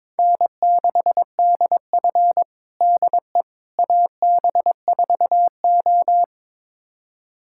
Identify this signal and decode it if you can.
Morse code, heard as a keyed tone, about 22 wpm, 700 Hz. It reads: N6DF DE AB4O
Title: USA Callsign DE Callsign - 4 Characters - 22wpm